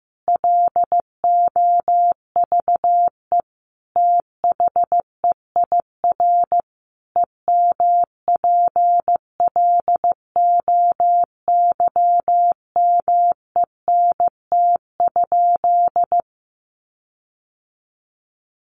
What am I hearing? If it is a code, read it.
Morse code, heard as a keyed tone, about 14 wpm, 700 Hz. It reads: LOVE THEIR EMPLOYMENT?
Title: Sentences from Top 1500 Words 15wpm